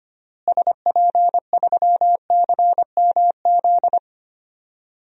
Morse code, heard as a keyed tone, about 25 wpm, 700 Hz. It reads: SP3CMZ
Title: DXCC-100 Call Signs - 6 Characters - Repeated 2x - 25wpm